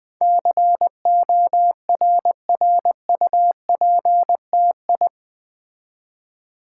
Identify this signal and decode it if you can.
Morse code, heard as a keyed tone, about 20 wpm, 700 Hz. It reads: CORRUPTI
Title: Build A Word - 4 Letters At A Time - 10-Letter Words - 20wpm